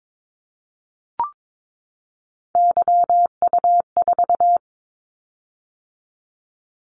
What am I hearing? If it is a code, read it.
Morse code, heard as a keyed tone, about 22 wpm, 700 Hz. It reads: YU4
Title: Callsign Prefixes for Top 25 DXCC - 22wpm